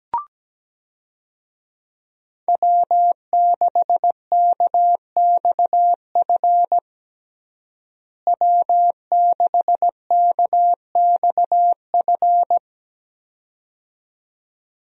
Morse code, heard as a keyed tone, about 17 wpm, 700 Hz. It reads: W6KXF W6KXF
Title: USA Call Signs - 5 Characters - Repeated 2x - 17wpm